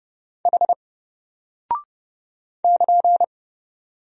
Morse code, heard as a keyed tone, about 30 wpm, 700 Hz. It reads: H <KN>
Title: Single Character (Letter:Number:Punctuation:Prosign) Alt 30wpm